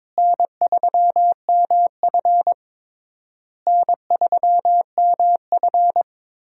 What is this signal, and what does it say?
Morse code, heard as a keyed tone, about 22 wpm, 700 Hz. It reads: N3MF N3MF
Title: USA Call Signs - 4 Characters - Repeated 3x - 22wpm